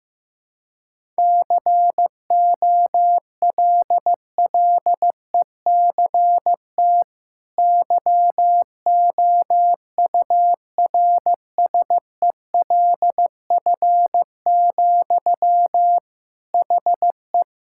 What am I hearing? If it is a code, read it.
Morse code, heard as a keyed tone, about 15 wpm, 700 Hz. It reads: COLLECT YOURSELF, HE
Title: Sentences from Top 1600 Words 15wpm